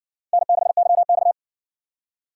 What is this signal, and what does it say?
Morse code, heard as a keyed tone, about 60 wpm, 700 Hz. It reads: N6YX